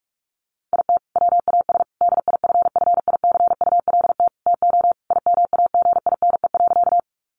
Morse code, heard as a keyed tone, about 44 wpm, 700 Hz. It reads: IT WAS DIFFICULT TO IMAGINE.
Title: Sentences from Top 1400 Words 45wpm